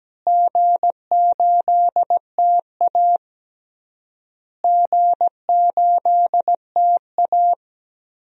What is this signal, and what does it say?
Morse code, heard as a keyed tone, about 17 wpm, 700 Hz. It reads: G8TA G8TA
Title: DXCC-100 Call Signs - 4 Characters - Repeated 3x - 17wpm